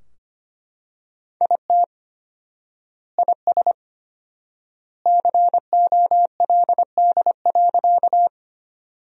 Morse code, heard as a keyed tone, about 25 wpm, 700 Hz. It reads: IT IS COLD.
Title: 4x Word-Spacing - Sentences from Top 400 Words - 25wpm